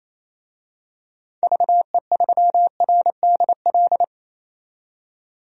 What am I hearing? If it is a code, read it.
Morse code, heard as a keyed tone, about 28 wpm, 700 Hz. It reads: VE3RDL
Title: DXCC-100 Call Signs - 6 Characters - 28wpm